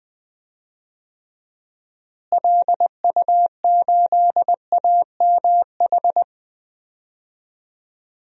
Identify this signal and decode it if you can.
Morse code, heard as a keyed tone, about 20 wpm, 700 Hz. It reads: LU8AMH
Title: DXCC-100 Call Signs - 6 Characters - Repeated 2x - 20wpm